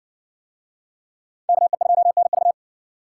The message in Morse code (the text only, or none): KE2NV